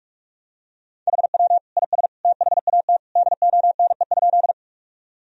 Morse code, heard as a keyed tone, about 41 wpm, 700 Hz. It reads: HW IS THAT DONE?
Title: Sentences from Top 500 Words and Abbreviations 45wpm